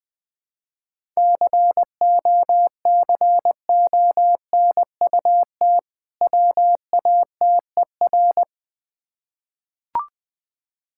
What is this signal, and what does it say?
Morse code, heard as a keyed tone, about 20 wpm, 690 Hz. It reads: COCONUT WATER
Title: Difficult Words with Hints - 20wpm